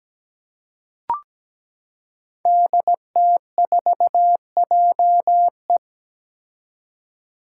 Morse code, heard as a keyed tone, about 17 wpm, 700 Hz. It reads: DT4JE
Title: Five Challenging Letter-Number Combinations 17wpm